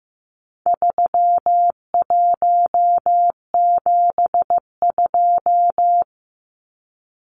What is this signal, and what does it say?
Morse code, heard as a keyed tone, about 15 wpm, 700 Hz. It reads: 3172
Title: Numbers - 4-Digit 15wpm